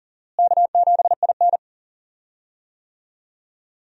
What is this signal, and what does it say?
Morse code, heard as a keyed tone, about 39 wpm, 700 Hz. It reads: K7IN
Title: USA Call Signs - 4 Characters - Repeated 2x - 40wpm